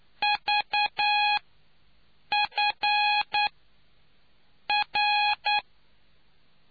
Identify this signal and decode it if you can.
Morse code, heard as a keyed tone, about 9 wpm, 830 Hz. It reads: V F R